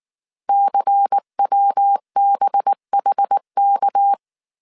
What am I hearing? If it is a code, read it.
Morse code, heard as a keyed tone, about 19 wpm, 800 Hz. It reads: CWBHK
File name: CW / DET / 1 9 1500C S A 18wpm 2459